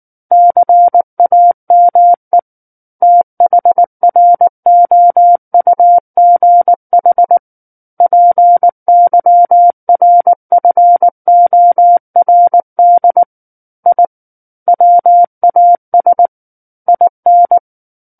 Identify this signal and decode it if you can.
Morse code, heard as a keyed tone, about 19 wpm, 700 Hz. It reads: CAME THROUGH PYRFORD I WAS IN
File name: Book One / 10-Chapter 10 - 19 WPM